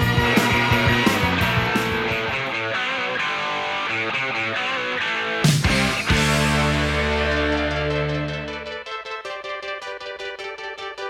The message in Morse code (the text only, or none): none